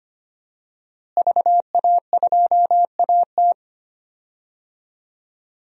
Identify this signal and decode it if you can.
Morse code, heard as a keyed tone, about 25 wpm, 700 Hz. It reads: VA2AT
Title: DXCC-100 Call Signs - 5 Characters - Repeated 2x - 25wpm